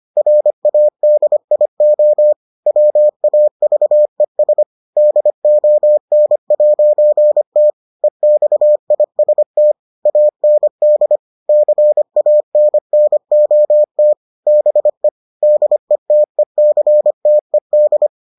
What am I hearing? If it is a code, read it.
Morse code, heard as a keyed tone, about 25 wpm, 600 Hz. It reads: RADIO WAVES DON'T EXIST AND CANNOT BE DETECTED